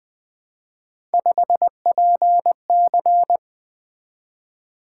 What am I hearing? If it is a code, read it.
Morse code, heard as a keyed tone, about 20 wpm, 700 Hz. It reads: 5PC
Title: Three Letter-Number Combinations - 20wpm